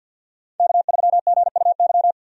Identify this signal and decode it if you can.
Morse code, heard as a keyed tone, about 51 wpm, 700 Hz. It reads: K3GUY
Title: USA Call Signs - 5 Characters - 50wpm